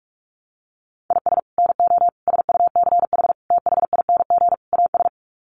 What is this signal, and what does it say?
Morse code, heard as a keyed tone, about 44 wpm, 700 Hz. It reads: IS NO SUCH THING AS
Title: Sets of 5 Words Encore 45wpm